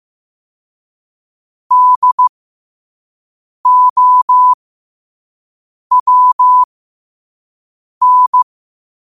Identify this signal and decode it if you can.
Morse code, heard as a keyed tone, about 15 wpm, spaced as for 5 wpm, 1000 Hz. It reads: DOWN
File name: qotdfiles / 5wpm